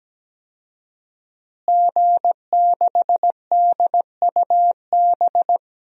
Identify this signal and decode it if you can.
Morse code, heard as a keyed tone, about 17 wpm, 700 Hz. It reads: G6DUB
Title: DXCC-100 Call Signs - 5 Characters - Repeated 2x - 17wpm